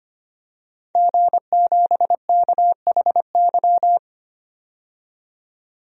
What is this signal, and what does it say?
Morse code, heard as a keyed tone, about 25 wpm, 700 Hz. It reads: G7KHY
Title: DXCC-100 Call Signs - 5 Characters - 25wpm